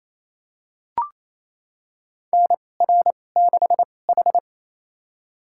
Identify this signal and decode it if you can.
Morse code, heard as a keyed tone, about 28 wpm, spaced as for 20 wpm, 700 Hz. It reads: NR6H
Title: USA Call Signs - 2x Character Spacing - 28wpm